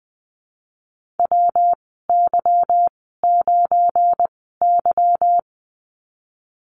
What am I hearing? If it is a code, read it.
Morse code, heard as a keyed tone, about 20 wpm, spaced as for 15 wpm, 700 Hz. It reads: WY9Y
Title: 179 Lesson 29g - Slash - Call Signs 20wpm